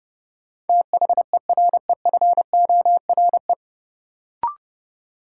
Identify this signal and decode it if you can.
Morse code, heard as a keyed tone, about 30 wpm, 700 Hz. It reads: THEREFORE
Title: Sentences from Top 800 Words - New Word Drill - 30wpm